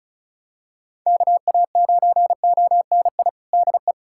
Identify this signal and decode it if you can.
Morse code, heard as a keyed tone, about 34 wpm, 700 Hz. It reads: KA9ONI DE
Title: USA Callsign DE Callsign - 6 Characters - 35wpm